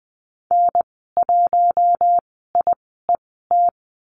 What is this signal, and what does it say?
Morse code, heard as a keyed tone, about 20 wpm, spaced as for 14 wpm, 700 Hz. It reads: N1IET
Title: 062 Lesson 11g - Letter h - Call Signs 20wpm